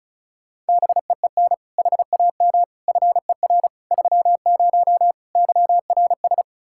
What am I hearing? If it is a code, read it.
Morse code, heard as a keyed tone, about 44 wpm, 700 Hz. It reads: BEEN HAM FER 30 YRS